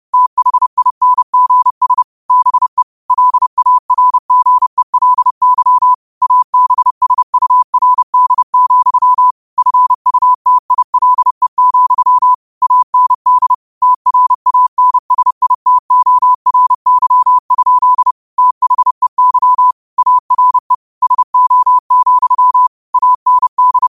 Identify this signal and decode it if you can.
Morse code, heard as a keyed tone, about 30 wpm, 1000 Hz. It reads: THINGS BE LARGELY ABSURD, FUTILE, AND TRANSITORY? THEY ARE SO, AND
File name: 30wpm